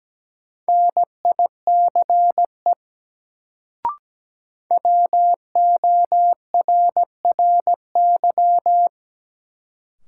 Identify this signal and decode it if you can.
Morse code, heard as a keyed tone, about 17 wpm, 700 Hz. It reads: NICE WORRY
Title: Sentences from Top 1100 Words - New Word Drill Alt - 17wpm